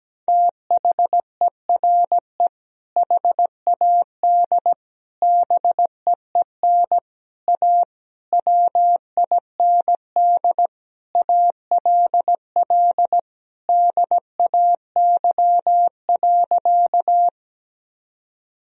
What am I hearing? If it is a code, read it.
Morse code, heard as a keyed tone, about 17 wpm, 700 Hz. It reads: THERE HAD BEEN A WIND ALL DAY.
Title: Sentences from Top 1000 Words - Review Part 2 - 17wpm